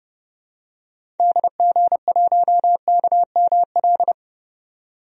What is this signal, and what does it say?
Morse code, heard as a keyed tone, about 30 wpm, 700 Hz. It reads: DG1KML
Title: DXCC-100 Call Signs - 6 Characters - 30wpm